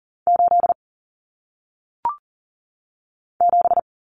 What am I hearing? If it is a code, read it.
Morse code, heard as a keyed tone, about 40 wpm, 700 Hz. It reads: 8 7
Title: Common Cut Numbers - 1-Digit 40wpm